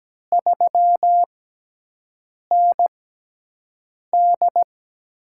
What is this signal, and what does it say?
Morse code, heard as a keyed tone, about 17 wpm, spaced as for 6 wpm, 700 Hz. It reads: 3ND
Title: USA Call Signs - 6x Character Spacing - 17wpm